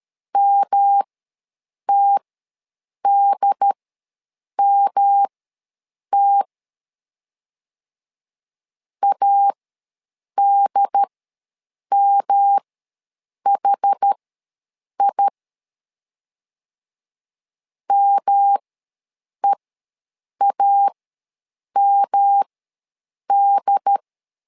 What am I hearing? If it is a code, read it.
Morse code, heard as a keyed tone, about 13 wpm, spaced as for 7 wpm, 800 Hz. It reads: MTDMT ADMHI MEAMD
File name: LEZIONI / 3 MWRO / 3 M ws 12wpm 1745